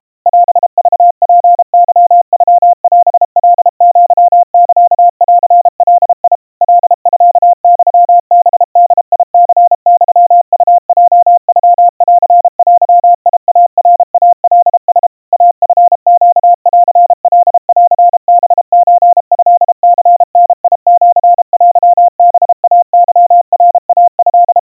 Morse code (Japanese Wabun code), heard as a keyed tone, about 32 wpm, 700 Hz. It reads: オクツケノオカアサンガ カミユハボニユウヲノンデイナイカラ イチネンカンハソトニダシテハイケナイト
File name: kawa / Morse sample / kamyu 120 cpm 700 Hz Crispy